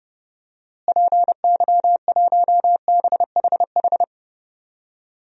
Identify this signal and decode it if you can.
Morse code, heard as a keyed tone, about 32 wpm, 700 Hz. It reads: PY1BHH
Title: DXCC-100 Call Signs - 6 Characters - Repeated 2x - 30wpm